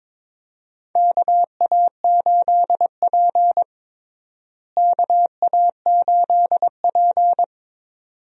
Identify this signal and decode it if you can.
Morse code, heard as a keyed tone, about 22 wpm, 700 Hz. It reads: KA8P KA8P
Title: USA Call Signs - 4 Characters - Repeated 2x - 22wpm